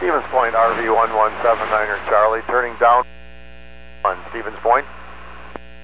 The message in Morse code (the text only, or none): none